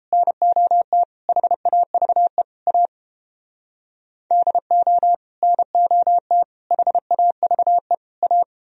Morse code, heard as a keyed tone, about 29 wpm, 700 Hz. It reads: NOT HAVE A DO NOT HAVE A